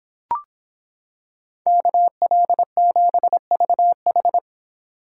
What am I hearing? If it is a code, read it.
Morse code, heard as a keyed tone, about 26 wpm, 700 Hz. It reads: KL7VH